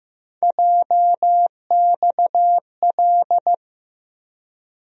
Morse code, heard as a keyed tone, about 15 wpm, 700 Hz. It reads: JXL
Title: USA Call Sign Postfixes - 3 Letters 15wpm